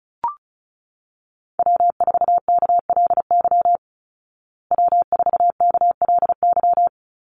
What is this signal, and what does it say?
Morse code, heard as a keyed tone, about 34 wpm, 700 Hz. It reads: W4KLY W4KLY
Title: Call Signs Repeated 2x - 35wpm